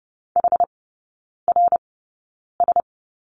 Morse code, heard as a keyed tone, about 30 wpm, spaced as for 9 wpm, 700 Hz. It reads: HRS
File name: Lesson 16e - Letter c - Three Characters 30wpm